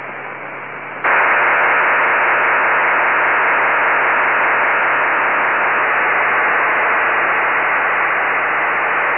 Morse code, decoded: none

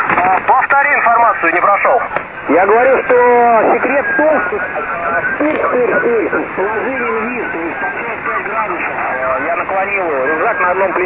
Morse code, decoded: none

